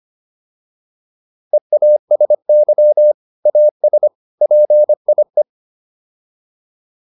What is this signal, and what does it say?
Morse code, heard as a keyed tone, about 25 wpm, 600 Hz. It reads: EASY AS PIE